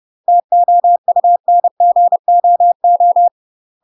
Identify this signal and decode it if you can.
Morse code, heard as a keyed tone, about 30 wpm, 700 Hz. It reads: TOUNGOO
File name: UK MASTER QSOs TRAINING / 30WPM / UK MASTER QSOs 0008